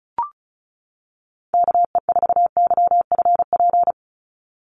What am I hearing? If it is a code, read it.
Morse code, heard as a keyed tone, about 34 wpm, 700 Hz. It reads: KE4YFP
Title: USA Call Signs 35wpm